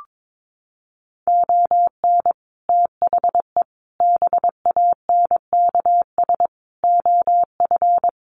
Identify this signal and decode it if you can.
Morse code, heard as a keyed tone, about 22 wpm, 700 Hz. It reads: ON THE BANKS OF
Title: Sets of 5 Words Encore2 22wpm